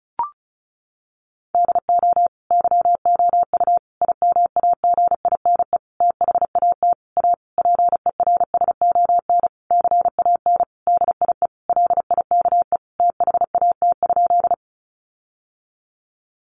Morse code, heard as a keyed tone, about 34 wpm, 700 Hz. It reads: DO YOU IMAGINE THAT A PERSON CAN DIE LIKE THAT?
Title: Sentences from Top 1400 Words 35wpm